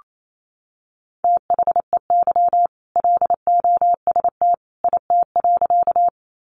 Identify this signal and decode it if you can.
Morse code, heard as a keyed tone, about 28 wpm, 700 Hz. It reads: THEY LOST IT.
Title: Sentences from Top 1400 Words 28wpm